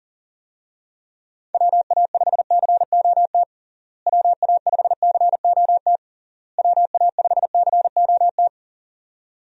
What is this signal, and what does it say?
Morse code, heard as a keyed tone, about 37 wpm, 700 Hz. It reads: WA5COT WA5COT WA5COT
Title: USA Call Signs - 6 Characters - Repeated 3x - 40wpm